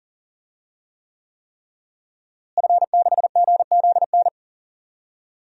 Morse code, heard as a keyed tone, about 40 wpm, 700 Hz. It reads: F6GZN
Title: DXCC-100 Call Signs - 5 Characters - Repeated 3x - 40wpm